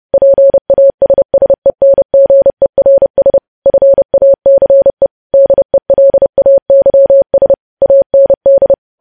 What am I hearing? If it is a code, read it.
Morse code, heard as a keyed tone, about 30 wpm, 550 Hz. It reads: PASSENGERS FACE DELAYS AND
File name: podcast / News headlines-1x-30x30